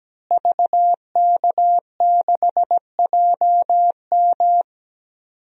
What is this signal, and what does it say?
Morse code, heard as a keyed tone, about 17 wpm, 700 Hz. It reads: VK6JM
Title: DXCC-100 Call Signs - 5 Characters - Repeated 2x - 17wpm